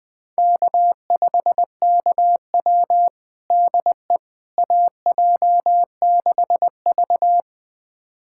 Morse code, heard as a keyed tone, about 20 wpm, 700 Hz. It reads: K5KW DE AJ6V
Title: USA Callsign DE Callsign - 4 Characters - 20wpm